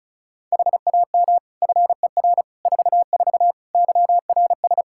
Morse code, heard as a keyed tone, about 36 wpm, 700 Hz. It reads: HAM FER 44 YRS
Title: Practice QSO - Exchange 3 - More Info - 35wpm